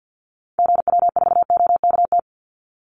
Morse code, heard as a keyed tone, about 49 wpm, 700 Hz. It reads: GW4OKT
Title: Call Signs Repeated 2x - 50wpm